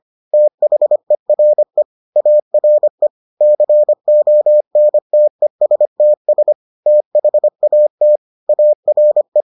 Morse code, heard as a keyed tone, about 25 wpm, 600 Hz. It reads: THERE ARE CONTESTS THAT ARE